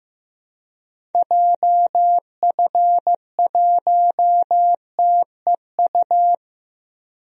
Morse code, heard as a keyed tone, about 14 wpm, 700 Hz. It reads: JF1TEU
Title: DXCC-100 Call Signs - 6 Characters - Repeated 2x - 15wpm